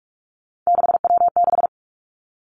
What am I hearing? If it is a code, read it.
Morse code, heard as a keyed tone, about 46 wpm, 700 Hz. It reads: 6W6
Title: Three Challenging Letter-Number Combinations 45wpm